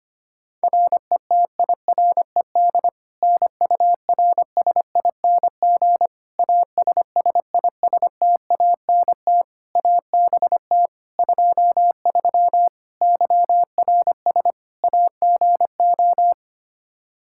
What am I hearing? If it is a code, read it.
Morse code, heard as a keyed tone, about 25 wpm, 700 Hz. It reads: RETIRED NURSING ASSISTANT ABT 23 YRS AGO
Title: Practice QSO - Exchange 3 - More Info - 25wpm